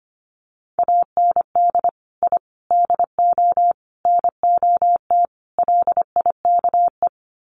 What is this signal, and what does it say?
Morse code, heard as a keyed tone, about 25 wpm, 700 Hz. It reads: AND I DO NOT LIKE
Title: Sets of 5 Words Encore2 25wpm